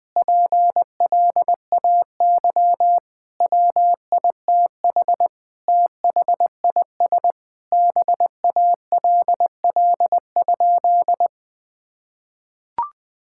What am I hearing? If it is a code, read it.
Morse code, heard as a keyed tone, about 20 wpm, 700 Hz. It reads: PLAY WITH THIS BALL?
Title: Sentences from Top 1000 Words - Review Part 6 - 20wpm